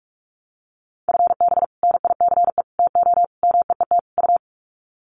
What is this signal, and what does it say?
Morse code, heard as a keyed tone, about 44 wpm, 700 Hz. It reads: FB NICE TO MEET U
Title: Common QSO Phrases 45wpm